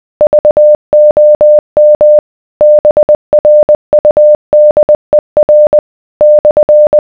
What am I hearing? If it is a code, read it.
Morse code, heard as a keyed tone, about 20 wpm, 600 Hz. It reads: VOM BRUDER /